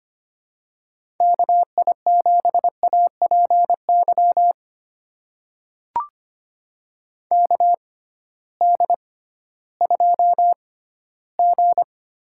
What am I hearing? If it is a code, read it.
Morse code, heard as a keyed tone, about 25 wpm, spaced as for 14 wpm, 700 Hz. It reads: KI7APY KD2G
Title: USA Call Signs - 6 Characters - 6x Character Spacing - 25wpm